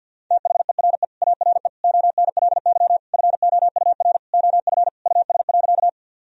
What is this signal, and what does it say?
Morse code, heard as a keyed tone, about 50 wpm, 700 Hz. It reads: THERE ARE ONLY FOUR OF US.
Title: Sentences from Top 1000 Words - Review Part 1 - 50wpm